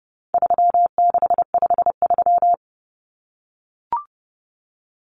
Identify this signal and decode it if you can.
Morse code, heard as a keyed tone, about 30 wpm, 700 Hz. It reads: 3653
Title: Common Cut Numbers - 4-Digit 30wpm